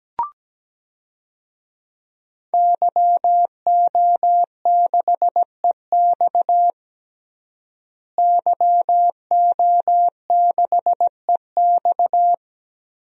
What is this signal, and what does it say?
Morse code, heard as a keyed tone, about 17 wpm, 700 Hz. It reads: YO6EX YO6EX
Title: DXCC-100 Call Signs - 5 Characters - Repeated 3x - 17wpm